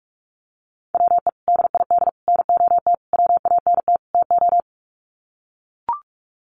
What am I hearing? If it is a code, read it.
Morse code, heard as a keyed tone, about 44 wpm, 700 Hz. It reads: WE DID NOT WANT TO
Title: Sets of 5 Words Encore 45wpm